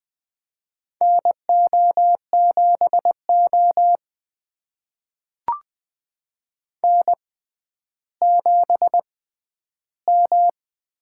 Morse code, heard as a keyed tone, about 20 wpm, spaced as for 11 wpm, 700 Hz. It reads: NO7O N7M
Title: USA Call Signs - 4 Characters - 6x Character Spacing - 20wpm